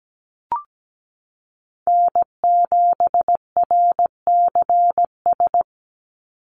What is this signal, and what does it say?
Morse code, heard as a keyed tone, about 17 wpm, 700 Hz. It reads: N7RCS
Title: Call Signs Repeated 3x - 17wpm